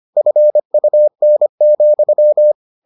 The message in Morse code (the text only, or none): FUN,